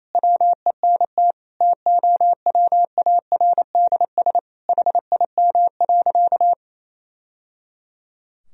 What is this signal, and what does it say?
Morse code, heard as a keyed tone, about 28 wpm, 700 Hz. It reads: WENT TOWARDS HIM.
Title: Sentences from Top 1000 Words - Review Part 2 - 28wpm